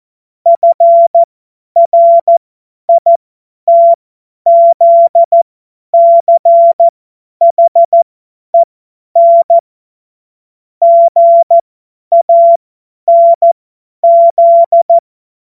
Morse code, heard as a keyed tone, about 14 wpm, spaced as for 10 wpm, 680 Hz. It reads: FRITZCHEN GANZ
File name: Morsekurs / Witze / Witz 15. August 2023 50 F70